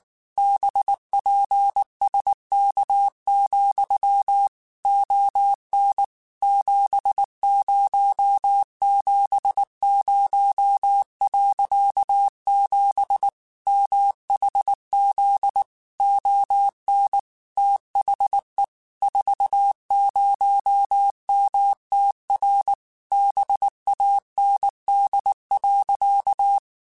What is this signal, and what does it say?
Morse code, heard as a keyed tone, about 19 wpm, 800 Hz. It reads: BPSK, ON 7070.7 MHZ ON THE 40MTR BAND.
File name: QSOs / QSOs-mp3s / kc7haa@18